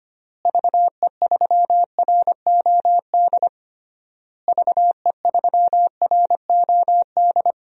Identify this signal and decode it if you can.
Morse code, heard as a keyed tone, about 25 wpm, 700 Hz. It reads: VE3ROD VE3ROD
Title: DXCC-100 Call Signs - 6 Characters - Repeated 3x - 25wpm